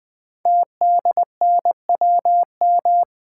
Morse code, heard as a keyed tone, about 20 wpm, 700 Hz. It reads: TDNWM
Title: Code Groups - 5 Letters - 20wpm